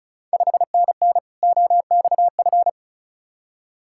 Morse code, heard as a keyed tone, about 36 wpm, 700 Hz. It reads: 5NN OXF
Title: Ontario QSO Party Exchange 35wpm